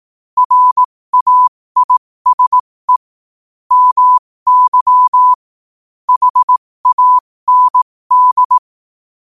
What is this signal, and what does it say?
Morse code, heard as a keyed tone, about 18 wpm, spaced as for 15 wpm, 1000 Hz. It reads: RAISE MY HAND
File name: qotdfiles / 15wpm